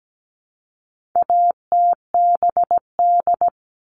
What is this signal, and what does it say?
Morse code, heard as a keyed tone, about 17 wpm, 700 Hz. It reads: ATBD
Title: All Cut Numbers - 4-Digit 17wpm